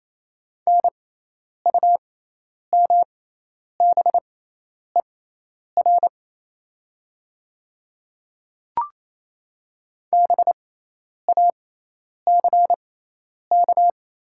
Morse code, heard as a keyed tone, about 28 wpm, spaced as for 10 wpm, 690 Hz. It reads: NUMBER BACK
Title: 6x Character Spacing - Top 200 Words - 28wpm